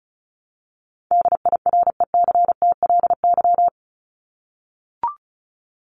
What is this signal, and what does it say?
Morse code, heard as a keyed tone, about 34 wpm, 700 Hz. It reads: DIRECTLY
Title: Top 501-750 Words 35wpm